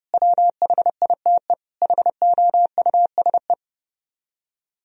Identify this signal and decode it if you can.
Morse code, heard as a keyed tone, about 37 wpm, 700 Hz. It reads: WHITE HOUSE
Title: Speed-Racing - Sets of 4 Words 30wpm